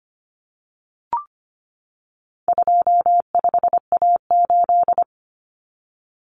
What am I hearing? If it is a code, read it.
Morse code, heard as a keyed tone, about 25 wpm, 700 Hz. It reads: 25A8
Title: Common Cut Numbers - 4-Digit 25wpm